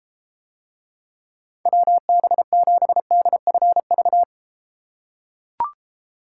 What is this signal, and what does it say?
Morse code, heard as a keyed tone, about 33 wpm, 700 Hz. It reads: WB7DFV